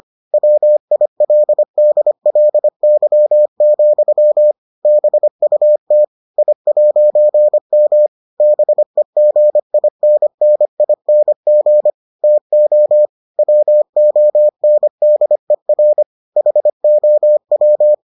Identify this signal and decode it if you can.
Morse code, heard as a keyed tone, about 25 wpm, 600 Hz. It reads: WILDLY, BUT I'M BEGINNING TO WONDER HOW